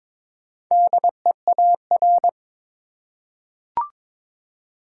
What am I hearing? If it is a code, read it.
Morse code, heard as a keyed tone, about 22 wpm, 700 Hz. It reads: DEAR
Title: Four Letter Words - 22wpm